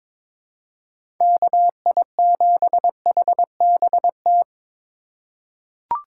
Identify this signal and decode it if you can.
Morse code, heard as a keyed tone, about 22 wpm, 700 Hz. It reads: KI7HBT
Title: USA Call Signs - 6 Characters - 3x Character Spacing - 22wpm